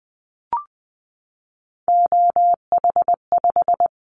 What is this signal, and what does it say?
Morse code, heard as a keyed tone, about 20 wpm, 700 Hz. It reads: OH5
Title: Callsign Prefixes for Top 50 DXCC - 20wpm